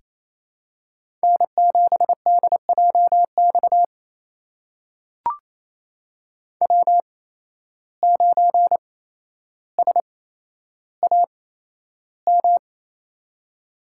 Morse code, heard as a keyed tone, about 28 wpm, spaced as for 12 wpm, 700 Hz. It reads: N7DJX W9SAM
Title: USA Call Signs - 5 Characters - 8x Character Spacing - 28wpm